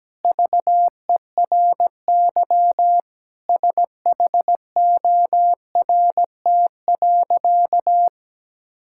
Morse code, heard as a keyed tone, about 17 wpm, 700 Hz. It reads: VERY SHORT.
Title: Sentences from Top 1600 Words 17wpm